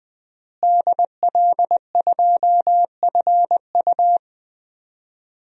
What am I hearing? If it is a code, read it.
Morse code, heard as a keyed tone, about 20 wpm, 700 Hz. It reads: DL2FU
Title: DXCC-100 Call Signs - 5 Characters - Repeated 2x - 20wpm